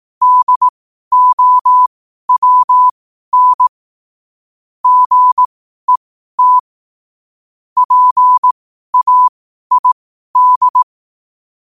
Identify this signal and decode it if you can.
Morse code, heard as a keyed tone, about 18 wpm, spaced as for 12 wpm, 1000 Hz. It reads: DOWN GET PAID
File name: qotdfiles / 12wpm